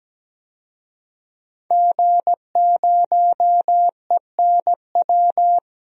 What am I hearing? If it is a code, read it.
Morse code, heard as a keyed tone, about 17 wpm, 700 Hz. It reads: G0ENW
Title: DXCC-100 Call Signs - 5 Characters - 17wpm